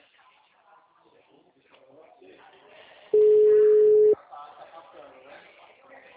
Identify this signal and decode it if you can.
Ringback tone (Europe, 425 Hz), with speech.
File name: out- -2004- - - .23994